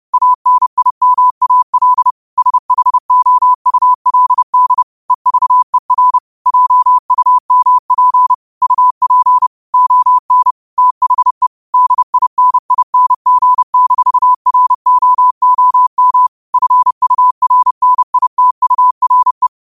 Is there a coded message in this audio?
Yes: Morse code from 0.1 to 19.5 s, heard as a keyed tone, about 30 wpm, 1000 Hz. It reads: ANIMAL SHOULD EVER JUMP UP ON THE DINING-ROOM FURNITURE